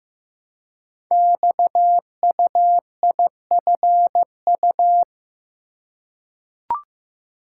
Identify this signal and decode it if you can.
Morse code, heard as a keyed tone, about 15 wpm, 700 Hz. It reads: XUIFU
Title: Code Groups - 5 Letters - 15wpm